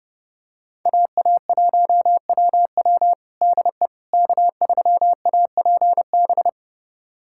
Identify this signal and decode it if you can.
Morse code, heard as a keyed tone, about 30 wpm, 700 Hz. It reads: AA1WW DE K3APB
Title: USA Callsign DE Callsign - 5 Characters - 30wpm